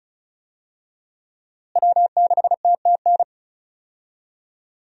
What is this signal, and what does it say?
Morse code, heard as a keyed tone, about 36 wpm, 700 Hz. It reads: W6TTN
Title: USA Call Signs - 5 Characters - Repeated 2x - 35wpm